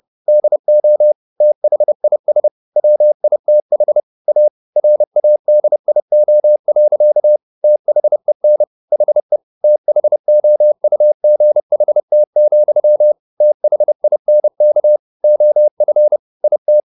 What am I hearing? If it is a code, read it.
Morse code, heard as a keyed tone, about 30 wpm, 600 Hz. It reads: DO THIS WITH A RADIO. THEN HE THOUGHT, THINK OF IT
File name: the old man and the sea / the old man and the sea 1 30wpm